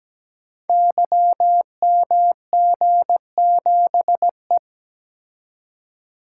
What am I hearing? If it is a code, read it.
Morse code, heard as a keyed tone, about 17 wpm, 700 Hz. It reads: YMG7E
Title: Code Groups - 5 Letter:Numbers - 17wpm